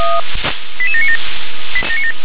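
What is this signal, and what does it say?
Five-tone selective call (ZVEI), heard as quick runs of five tones, tones of about 70 ms. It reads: 88797
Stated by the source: ZVEI five-tone selective call 88753 received by SDR.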